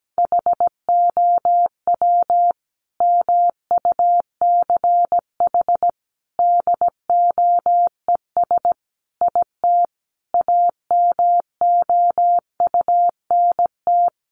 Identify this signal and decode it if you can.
Morse code, heard as a keyed tone, about 17 wpm, 700 Hz. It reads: HOW MUCH DOES IT AMOUNT
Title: Sentences from Top 700 Words 17wpm